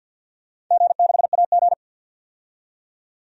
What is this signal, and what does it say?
Morse code, heard as a keyed tone, about 52 wpm, 700 Hz. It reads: G6AG
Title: DXCC-100 Call Signs - 4 Characters - 50wpm